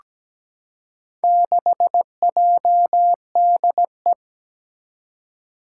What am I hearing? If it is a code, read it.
Morse code, heard as a keyed tone, about 17 wpm, 700 Hz. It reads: 6JDE
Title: Four Challenging Letter-Number Combinations 17wpm